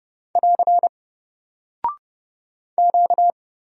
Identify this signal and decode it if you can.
Morse code, heard as a keyed tone, about 30 wpm, 700 Hz. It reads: <AR> Q
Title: Single Character (Letter:Number:Punctuation:Prosign) Alt 30wpm